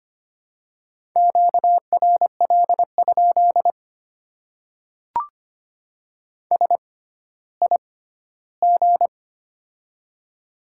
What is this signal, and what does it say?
Morse code, heard as a keyed tone, about 25 wpm, spaced as for 14 wpm, 690 Hz. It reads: QRL? SIG
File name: Top 90 QSO Elements - 6x Character Spacing - 25wpm